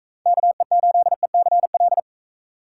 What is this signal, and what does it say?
Morse code, heard as a keyed tone, about 40 wpm, 700 Hz. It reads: KE8ECL